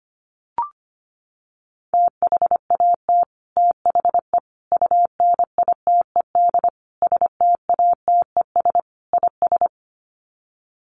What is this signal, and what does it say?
Morse code, heard as a keyed tone, about 25 wpm, 690 Hz. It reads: THAT THE UNITED STATES IS
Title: Sets of 5 Words Encore 25wpm